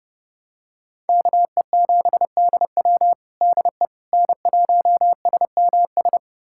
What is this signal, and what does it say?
Morse code, heard as a keyed tone, about 30 wpm, 700 Hz. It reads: KE7DW DE N1SMS
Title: USA Callsign DE Callsign - 5 Characters - 30wpm